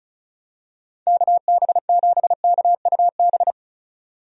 Morse code, heard as a keyed tone, about 36 wpm, 700 Hz. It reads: KB7KUB
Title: USA Call Signs - 6 Characters - 35wpm